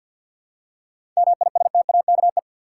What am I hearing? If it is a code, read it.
Morse code, heard as a keyed tone, about 52 wpm, 700 Hz. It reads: MISTAKE
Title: Seven Letter Words - 50wpm